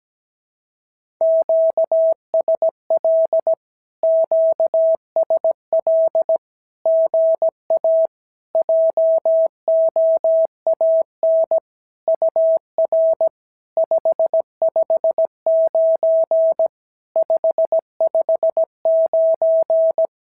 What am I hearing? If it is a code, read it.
Morse code, heard as a keyed tone, about 17 wpm, 650 Hz. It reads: QSL QSL GA JOAN UR 559 559
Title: SOTA QSO Practice 17wpm